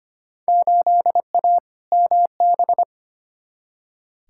Morse code, heard as a keyed tone, about 25 wpm, 700 Hz. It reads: 8A MB
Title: Field Day Exchanges Encore - 25wpm